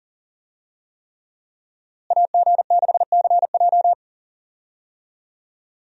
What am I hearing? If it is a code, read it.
Morse code, heard as a keyed tone, about 40 wpm, 700 Hz. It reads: AG6CJ
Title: USA Call Signs - 5 Characters - Repeated 3x - 40wpm